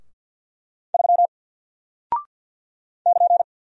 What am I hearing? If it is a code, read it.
Morse code, heard as a keyed tone, about 50 wpm, 700 Hz. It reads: 3 <KN>
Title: Single Character (Letter:Number:Punctuation:Prosign) Alt 50wpm